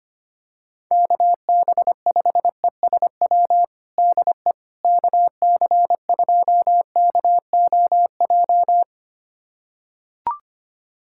Morse code, heard as a keyed tone, about 25 wpm, 700 Hz. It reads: KB5ESW DE KC2KOJ
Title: USA Callsign DE Callsign 25wpm